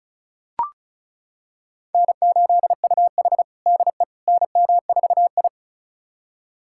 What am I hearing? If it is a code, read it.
Morse code, heard as a keyed tone, about 34 wpm, 700 Hz. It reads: N8UH DE NM4I
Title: USA Callsign DE Callsign - 4 Characters - 35wpm